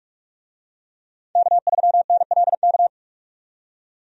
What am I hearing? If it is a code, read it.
Morse code, heard as a keyed tone, about 46 wpm, 700 Hz. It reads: K3NLK
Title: USA Call Signs - 5 Characters - 45wpm